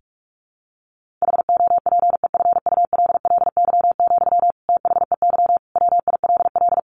Morse code, heard as a keyed tone, about 44 wpm, 700 Hz. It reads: HOPEFULLY, THEY WILL
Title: Sentences from Top 1500 Words 45wpm